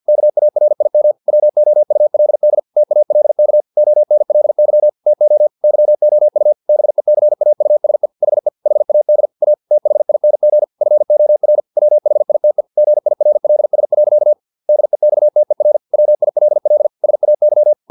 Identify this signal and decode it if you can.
Morse code, heard as a keyed tone, about 50 wpm, 600 Hz. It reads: KARIM WOULD TALK ONLY TO YOU BECAUSE HE HAD A THING FOR WHITE GIRLS. DEXTER WILL SAY
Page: Never 49wpm